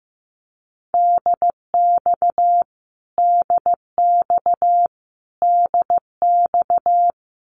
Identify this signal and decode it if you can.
Morse code, heard as a keyed tone, about 15 wpm, 700 Hz. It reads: DX DX DX
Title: Abbreviations Repeated 3x - 15wpm